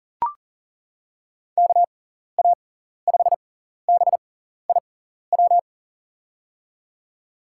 Morse code, heard as a keyed tone, about 40 wpm, spaced as for 14 wpm, 700 Hz. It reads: KA5BIW
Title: USA Call Signs - 6 Characters - 6x Character Spacing - 40wpm